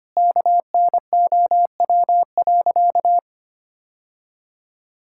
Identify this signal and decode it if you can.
Morse code, heard as a keyed tone, about 25 wpm, 700 Hz. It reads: KNOW.
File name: practice / 25wpm / Sentences with 2-Word Phrases - 25wpm